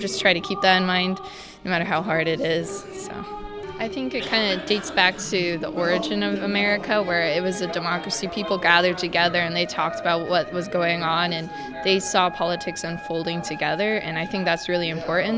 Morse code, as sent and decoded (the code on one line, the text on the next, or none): none